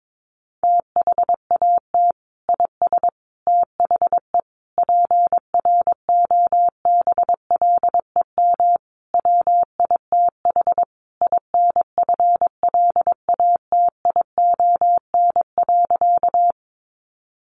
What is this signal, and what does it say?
Morse code, heard as a keyed tone, about 22 wpm, 700 Hz. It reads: THAT IS THE PROBLEM WITH INFLATION.
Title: Sentences from Top 1100 Words 22wpm